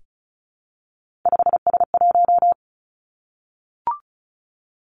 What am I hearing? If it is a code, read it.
Morse code, heard as a keyed tone, about 36 wpm, 700 Hz. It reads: 5S1
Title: Three Challenging Letter-Number Combinations 35wpm